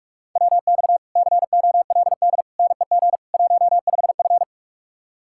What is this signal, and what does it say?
Morse code, heard as a keyed tone, about 48 wpm, 700 Hz. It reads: WX COLD NEG 15F